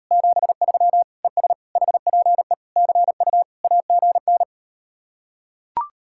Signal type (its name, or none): Morse code